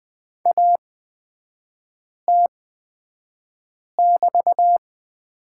Morse code, heard as a keyed tone, about 20 wpm, spaced as for 5 wpm, 700 Hz. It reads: AT<BT>
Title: 224 Lesson 36e - BT - Three Characters 20wpm